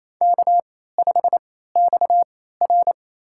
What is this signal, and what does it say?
Morse code, heard as a keyed tone, about 28 wpm, spaced as for 16 wpm, 700 Hz. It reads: K5XR
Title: USA Call Signs - 4 Characters - 3x Character Spacing - 28wpm